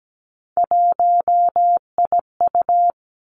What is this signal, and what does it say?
Morse code, heard as a keyed tone, about 17 wpm, 700 Hz. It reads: 1IU
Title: Three Challenging Letter-Number Combinations 17wpm